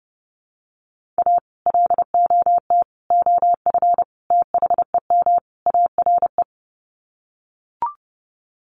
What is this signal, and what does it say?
Morse code, heard as a keyed tone, about 30 wpm, 700 Hz. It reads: A LOT OF THEM ARE
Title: Sets of 5 Words Encore2 30wpm